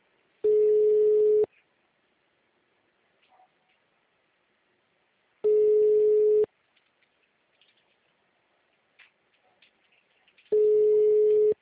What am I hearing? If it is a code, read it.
Ringback tone (Europe, 425 Hz), with speech.
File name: out- -2014- - - .16918